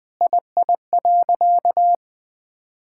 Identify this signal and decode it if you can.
Morse code, heard as a keyed tone, about 20 wpm, 700 Hz. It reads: II.
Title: Sentences with 6-Word Phrases - 20wpm